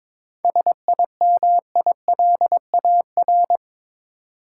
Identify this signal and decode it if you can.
Morse code, heard as a keyed tone, about 22 wpm, 700 Hz. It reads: SIMILAR
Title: Sentences from Top 1000 Words - Review Part 5 - New Word Drill - 22wpm